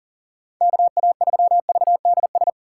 Morse code, heard as a keyed tone, about 40 wpm, 700 Hz. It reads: KA3VDS
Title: USA Call Signs - 6 Characters - 2x Character Spacing - 40wpm